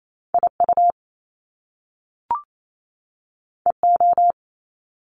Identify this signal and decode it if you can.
Morse code, heard as a keyed tone, about 28 wpm, 700 Hz. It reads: IU EO
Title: Top Two Letter Combinations in Words 28wpm